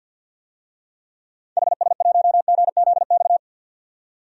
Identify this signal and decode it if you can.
Morse code, heard as a keyed tone, about 52 wpm, 700 Hz. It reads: HS1GZX
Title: DXCC-100 Call Signs - 6 Characters - 50wpm